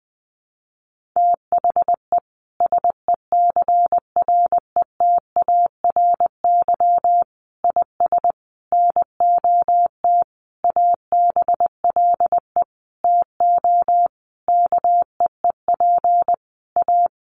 Morse code, heard as a keyed tone, about 20 wpm, 700 Hz. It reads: THE SECRETARY IS NOT ABLE TO KEEP A
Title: Sentences from Top 1300 Words 20wpm